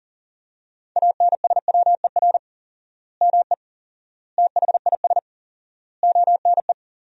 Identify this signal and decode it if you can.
Morse code, heard as a keyed tone, about 37 wpm, 700 Hz. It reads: ANSWER ME THIS ONE
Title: 4x Word-Spacing - Sentences from Top 400 Words - 40wpm